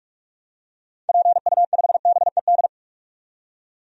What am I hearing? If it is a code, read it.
Morse code, heard as a keyed tone, about 42 wpm, 700 Hz. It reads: PU5BED
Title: DXCC-100 Call Signs - 6 Characters - 45wpm